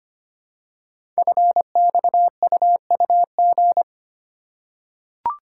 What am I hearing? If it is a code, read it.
Morse code, heard as a keyed tone, about 25 wpm, 700 Hz. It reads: FXUUG
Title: Code Groups - 5 Letters - 25wpm